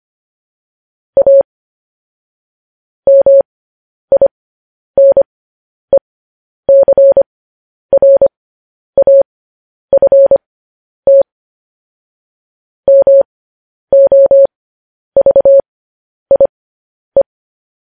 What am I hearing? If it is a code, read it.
Morse code, heard as a keyed tone, about 25 wpm, spaced as for 10 wpm, 550 Hz. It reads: A MINECRAFT MOVIE
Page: news-headlines-in-morse-code-at-10-wpm